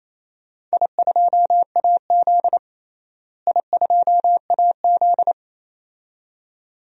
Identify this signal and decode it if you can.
Morse code, heard as a keyed tone, about 28 wpm, 700 Hz. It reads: I2AZ I2AZ
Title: DXCC-100 Call Signs - 4 Characters - Repeated 2x - 28wpm